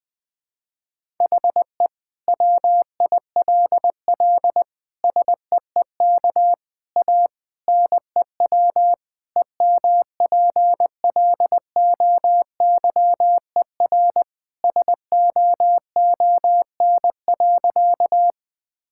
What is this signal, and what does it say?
Morse code, heard as a keyed tone, about 20 wpm, 700 Hz. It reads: HE WILL SEEK A NEW EMPLOYER SOON.
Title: Sentences from Top 1600 Words 20wpm